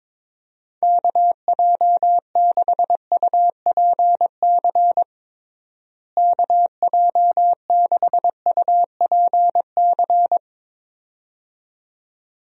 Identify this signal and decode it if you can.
Morse code, heard as a keyed tone, about 22 wpm, 700 Hz. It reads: KJ6UPC KJ6UPC
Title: USA Call Signs - 6 Characters - Repeated 2x - 22wpm